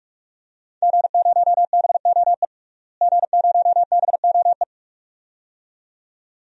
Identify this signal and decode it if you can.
Morse code, heard as a keyed tone, about 42 wpm, 700 Hz. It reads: G0BOE G0BOE
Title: DXCC-100 Call Signs - 5 Characters - Repeated 2x - 45wpm